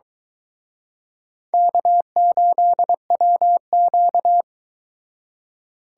Morse code, heard as a keyed tone, about 22 wpm, 700 Hz. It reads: K8WQ